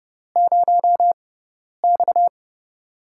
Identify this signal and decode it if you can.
Morse code, heard as a keyed tone, about 30 wpm, spaced as for 10 wpm, 700 Hz. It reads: 0X
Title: USA Call Signs - 6x Character Spacing - 30wpm